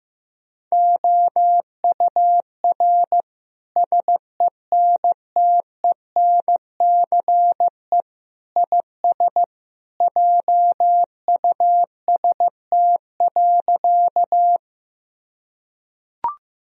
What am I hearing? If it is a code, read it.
Morse code, heard as a keyed tone, about 15 wpm, 700 Hz. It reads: OUR SENTENCE IS JUST.
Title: Sentences from Top 1000 Words - Review Part 1 - 15wpm